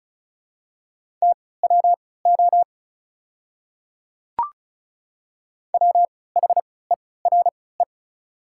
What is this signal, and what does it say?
Morse code, heard as a keyed tone, about 36 wpm, spaced as for 20 wpm, 700 Hz. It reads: TWO WHERE
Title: 3x Character Spacing - Top 200 Words - 35wpm